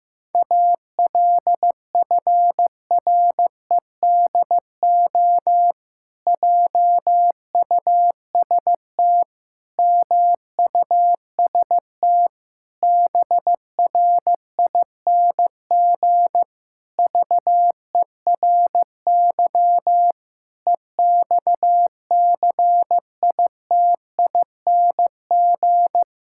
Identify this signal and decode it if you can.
Morse code, heard as a keyed tone, about 15 wpm, 700 Hz. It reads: ALFREDO JUST MUST BRING VERY EXCITING
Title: Pangram Challenge v2 - 15wpm